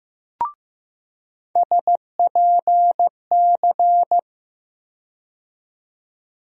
Morse code, heard as a keyed tone, about 15 wpm, 700 Hz. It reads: SPC
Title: USA Call Sign Postfixes - 3 Letters 15wpm